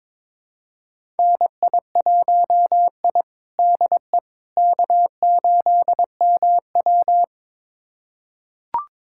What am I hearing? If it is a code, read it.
Morse code, heard as a keyed tone, about 22 wpm, 700 Hz. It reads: NI1I DE K8MW
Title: USA Callsign DE Callsign - 4 Characters - 22wpm